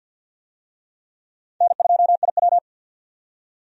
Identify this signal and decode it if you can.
Morse code, heard as a keyed tone, about 52 wpm, 700 Hz. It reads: N2IW
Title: USA Call Signs - 4 Characters - 50wpm